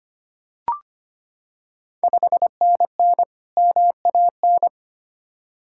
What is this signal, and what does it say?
Morse code, heard as a keyed tone, about 25 wpm, 700 Hz. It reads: 5NN MAN
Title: Ontario QSO Party Exchange 25wpm